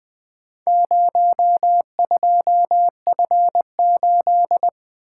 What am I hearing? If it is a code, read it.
Morse code, heard as a keyed tone, about 20 wpm, 700 Hz. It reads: 02F8
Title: Four Letter-Number Combinations - 20wpm